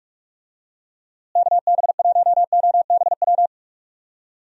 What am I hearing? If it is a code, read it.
Morse code, heard as a keyed tone, about 41 wpm, 700 Hz. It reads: KB1OBW
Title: USA Call Signs - 6 Characters - Repeated 3x - 45wpm